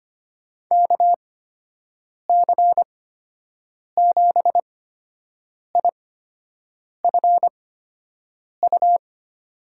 Morse code, heard as a keyed tone, about 25 wpm, spaced as for 7 wpm, 700 Hz. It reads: KC7IFU
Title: USA Call Signs - 6 Characters - 8x Character Spacing - 25wpm